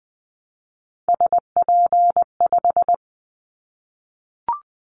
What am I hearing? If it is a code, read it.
Morse code, heard as a keyed tone, about 20 wpm, 700 Hz. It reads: SP5
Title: Callsign Prefixes for Top 25 DXCC - Repeated 3x - 20wpm